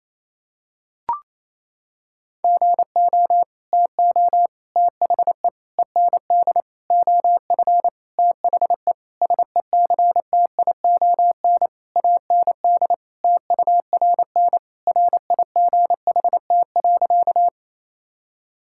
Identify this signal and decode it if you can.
Morse code, heard as a keyed tone, about 28 wpm, 700 Hz. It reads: GO TO THE END OF THE SECTION AND TURN RIGHT.
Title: Sentences from Top 1000 Words - Review Part 5 - 28wpm